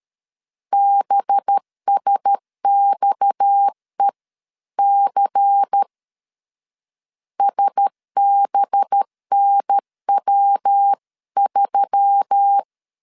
Morse code, heard as a keyed tone, about 13 wpm, 800 Hz. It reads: BSXEC SBNW3
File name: CW / DET / 1 6 1500C S A 12wpm 3440